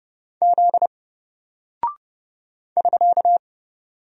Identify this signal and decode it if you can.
Morse code, heard as a keyed tone, about 30 wpm, 700 Hz. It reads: Z <SK>
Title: Single Character (Letter:Number:Punctuation:Prosign) 30wpm